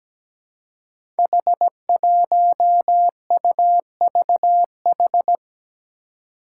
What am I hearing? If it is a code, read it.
Morse code, heard as a keyed tone, about 17 wpm, 700 Hz. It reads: H1UVH
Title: Five Challenging Letter-Number Combinations 17wpm